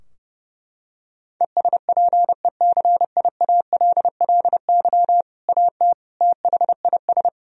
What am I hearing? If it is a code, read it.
Morse code, heard as a keyed tone, about 32 wpm, 700 Hz. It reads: ESPECIALLY AT THIS
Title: Sentences from Top 1000 Words - Review Part 4 - 30wpm